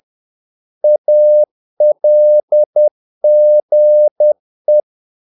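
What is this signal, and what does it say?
Morse code, heard as a keyed tone, about 10 wpm, 600 Hz. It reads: ALGE
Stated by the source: Morse practice of short Danish words at 10 wpm, from CW med Gnister.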